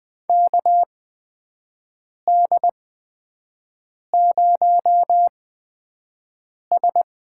Morse code, heard as a keyed tone, about 20 wpm, spaced as for 5 wpm, 700 Hz. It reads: KD0S
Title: USA Call Signs - 6 Characters - 8x Character Spacing - 20wpm